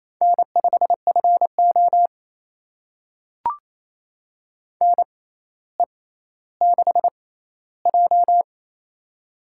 Morse code, heard as a keyed tone, about 28 wpm, spaced as for 14 wpm, 700 Hz. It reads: N5FO NE6J
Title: USA Call Signs - 6x Character Spacing - 28wpm